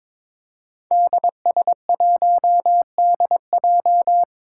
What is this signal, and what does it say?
Morse code, heard as a keyed tone, about 22 wpm, 700 Hz. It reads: DS1DJ
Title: Five Challenging Letter-Number Combinations 22wpm